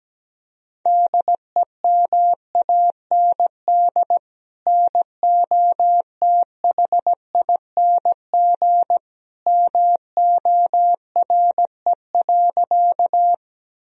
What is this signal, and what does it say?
Morse code, heard as a keyed tone, about 17 wpm, 700 Hz. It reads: DEMAND NOTHING MORE.
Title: Sentences from Top 1000 Words - Review Part 4 - 17wpm